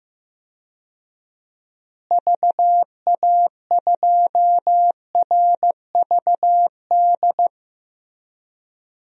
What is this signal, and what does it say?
Morse code, heard as a keyed tone, about 15 wpm, 700 Hz. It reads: VA2RVD
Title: DXCC-100 Call Signs - 6 Characters - Repeated 2x - 15wpm